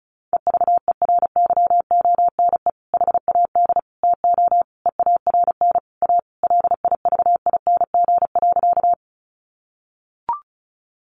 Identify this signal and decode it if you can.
Morse code, heard as a keyed tone, about 34 wpm, 700 Hz. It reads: EVERYONE HAD TO EARN A LIVING.
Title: Sentences from Top 1500 Words 35wpm